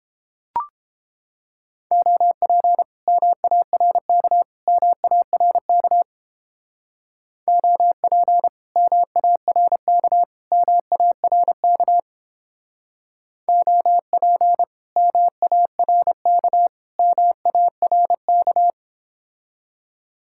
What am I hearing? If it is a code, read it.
Morse code, heard as a keyed tone, about 31 wpm, 700 Hz. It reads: OP MARK MARK OP MARK MARK OP MARK MARK